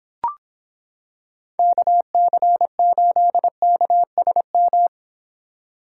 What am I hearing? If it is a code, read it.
Morse code, heard as a keyed tone, about 26 wpm, 700 Hz. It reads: KC8KSM